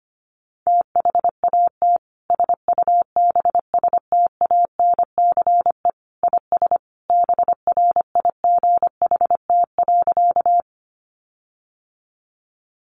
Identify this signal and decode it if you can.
Morse code, heard as a keyed tone, about 25 wpm, 700 Hz. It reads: THAT SUBSTANCE IS BRIGHT.
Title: Sentences from Top 1500 Words 25wpm